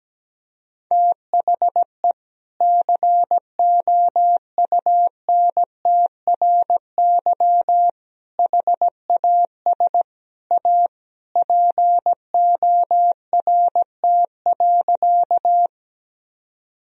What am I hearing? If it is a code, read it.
Morse code, heard as a keyed tone, about 17 wpm, 700 Hz. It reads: THE COUNTRY HAS A PORT.
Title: Sentences from Top 1000 Words - Review Part 1 - 17wpm